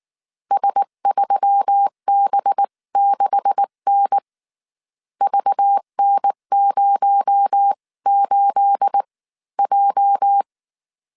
Morse code, heard as a keyed tone, about 19 wpm, spaced as for 16 wpm, 800 Hz. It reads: S3B6N VN08J
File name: CW / DET / 1 9 1500C S B 18wpm 2551